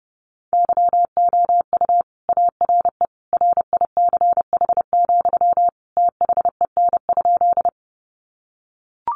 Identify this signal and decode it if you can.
Morse code, heard as a keyed tone, about 29 wpm, 700 Hz. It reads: YOU ARE RICH, THEN?
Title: Sentences from Top 1100 Words 30wpm